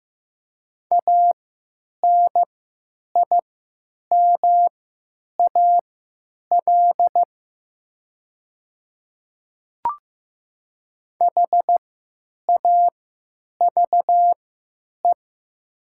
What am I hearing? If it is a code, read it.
Morse code, heard as a keyed tone, about 15 wpm, spaced as for 9 wpm, 700 Hz. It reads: ANIMAL HAVE
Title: 3x Character Spacing - Top 300 Words - 15wpm